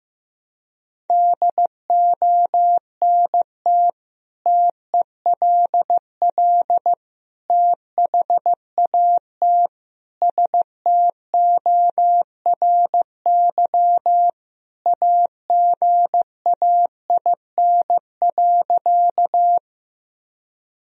Morse code, heard as a keyed tone, about 16 wpm, 700 Hz. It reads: DONT TELL THAT STORY AGAIN.
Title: Sentences from Top 1000 Words - Review Part 1 - 15wpm